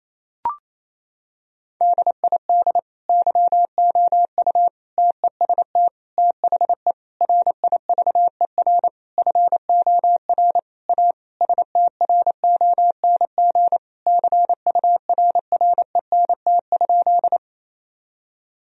Morse code, heard as a keyed tone, about 28 wpm, 700 Hz. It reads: DID YOU TEST THE RIVER FOR A STRONG CURRENT?
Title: Sentences from Top 1000 Words - Review Part 4 - 28wpm